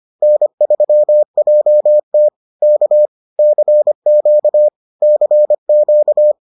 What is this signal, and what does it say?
Morse code, heard as a keyed tone, about 25 wpm, 600 Hz. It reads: N3JT K CQ CQ